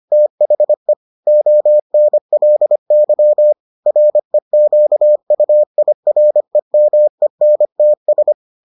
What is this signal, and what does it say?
Morse code, heard as a keyed tone, about 25 wpm, 600 Hz. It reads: THE ONLY REQUIREMENTS